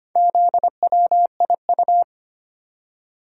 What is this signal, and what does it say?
Morse code, heard as a keyed tone, about 25 wpm, 700 Hz. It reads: ZWIU
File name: practice / 25wpm / Four Letter-Number Combinations - 25wpm